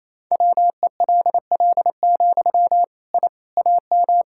Morse code, heard as a keyed tone, about 28 wpm, 700 Hz. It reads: WELL, I AM
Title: Sentences from Top 1000 Words - Review Part 5 - 28wpm